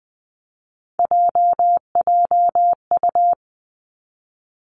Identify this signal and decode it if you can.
Morse code, heard as a keyed tone, about 20 wpm, 700 Hz. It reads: JJU
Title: Three Challenging Letter-Number Combinations 20wpm